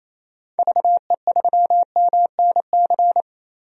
Morse code, heard as a keyed tone, about 28 wpm, 700 Hz. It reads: VE3MNC
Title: DXCC-100 Call Signs - 6 Characters - Repeated 3x - 28wpm